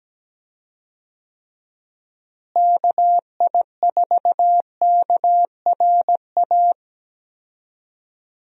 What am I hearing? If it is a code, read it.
Morse code, heard as a keyed tone, about 17 wpm, 700 Hz. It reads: KI4KRA
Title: USA Call Signs - 6 Characters - Repeated 3x - 17wpm